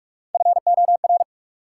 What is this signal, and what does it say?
Morse code, heard as a keyed tone, about 46 wpm, 700 Hz. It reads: FOR
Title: Sets of 2 Words - New Word Drill - 45wpm